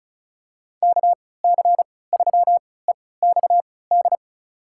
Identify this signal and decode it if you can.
Morse code, heard as a keyed tone, about 34 wpm, spaced as for 20 wpm, 700 Hz. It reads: KC3EXD
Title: USA Call Signs - 3x Character Spacing - 35wpm